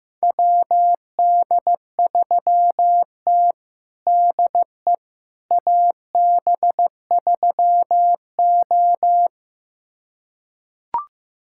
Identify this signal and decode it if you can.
Morse code, heard as a keyed tone, about 15 wpm, 700 Hz. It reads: WD3T DE AB3O
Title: USA Callsign DE Callsign - 4 Characters - 15wpm